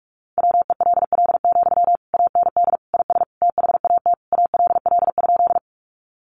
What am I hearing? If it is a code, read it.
Morse code, heard as a keyed tone, about 44 wpm, 700 Hz. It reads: WELL, AND IS THAT ALL?
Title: Sentences from Top 200 Words 45wpm